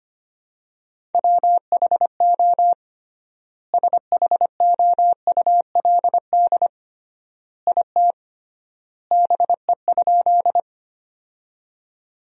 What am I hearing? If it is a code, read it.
Morse code, heard as a keyed tone, about 25 wpm, 700 Hz. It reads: WHO SHOULD IT BE?
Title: 3x Word-Spacing - Sentences from Top 300 Words - 25wpm